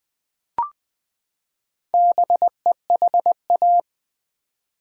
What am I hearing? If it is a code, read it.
Morse code, heard as a keyed tone, about 20 wpm, 700 Hz. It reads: BEHA
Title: Build A Word - 4 Letters At A Time - 10-Letter Words - 20wpm